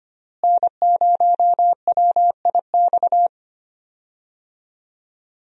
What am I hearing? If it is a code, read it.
Morse code, heard as a keyed tone, about 25 wpm, 700 Hz. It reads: N0WIX
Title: USA Call Signs - 5 Characters - Repeated 3x - 25wpm